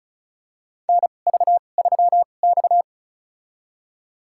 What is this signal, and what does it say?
Morse code, heard as a keyed tone, about 36 wpm, spaced as for 25 wpm, 700 Hz. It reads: NV3X
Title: USA Call Signs - 4 Characters - 2x Character Spacing - 35wpm